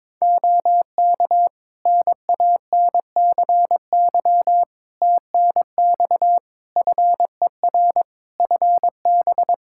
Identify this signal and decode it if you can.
Morse code, heard as a keyed tone, about 22 wpm, 700 Hz. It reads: OK NANCY TNX FER FB
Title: Practice QSO - Exchange 4 - Ending - 22wpm